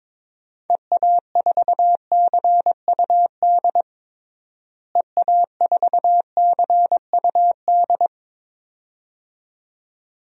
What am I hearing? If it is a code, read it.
Morse code, heard as a keyed tone, about 22 wpm, 700 Hz. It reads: EA4CUD EA4CUD
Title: DXCC-100 Call Signs - 6 Characters - Repeated 2x - 22wpm